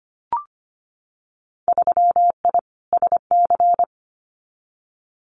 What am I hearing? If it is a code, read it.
Morse code, heard as a keyed tone, about 25 wpm, 700 Hz. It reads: 3I SC
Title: Winter Field Day Exchanges 25wpm